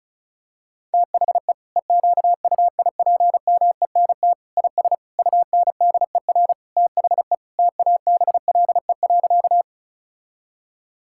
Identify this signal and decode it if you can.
Morse code, heard as a keyed tone, about 36 wpm, 700 Hz. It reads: THE EQUIPMENT IS UNDER THE TABLE.
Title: Sentences from Top 1000 Words - Review Part 5 - 35wpm